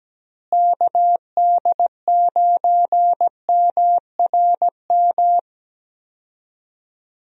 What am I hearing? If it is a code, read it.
Morse code, heard as a keyed tone, about 17 wpm, 700 Hz. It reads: KD9MRM
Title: USA Call Signs - 6 Characters - Repeated 2x - 17wpm